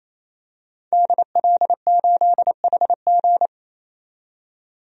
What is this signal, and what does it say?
Morse code, heard as a keyed tone, about 28 wpm, 700 Hz. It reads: DL8HG
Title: DXCC-100 Call Signs - 5 Characters - 28wpm